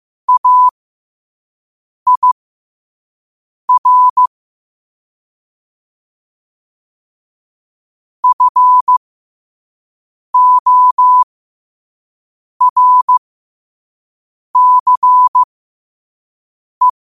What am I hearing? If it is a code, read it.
Morse code, heard as a keyed tone, about 15 wpm, spaced as for 5 wpm, 1000 Hz. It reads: AIR FORCE
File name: qotdfiles / 5wpm